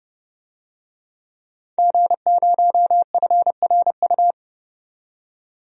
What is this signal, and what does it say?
Morse code, heard as a keyed tone, about 31 wpm, 700 Hz. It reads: G0FRU
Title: DXCC-100 Call Signs - 5 Characters - 30wpm